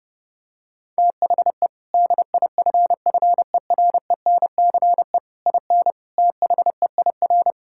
Morse code, heard as a keyed tone, about 30 wpm, 700 Hz. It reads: THE DIFFERENCE IN THEIR
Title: Sentences from Top 1600 Words 30wpm